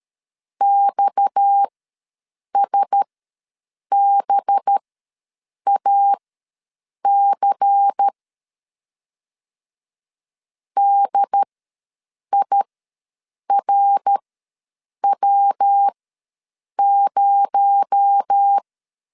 Morse code, heard as a keyed tone, about 13 wpm, spaced as for 7 wpm, 800 Hz. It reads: XSBAC DIRW0
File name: LEZIONI / 6 PKX0 / 6 PKX0 ws 12wpm 3448